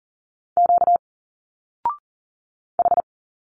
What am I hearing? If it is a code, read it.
Morse code, heard as a keyed tone, about 40 wpm, 700 Hz. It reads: Q H
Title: Single Letters 40wpm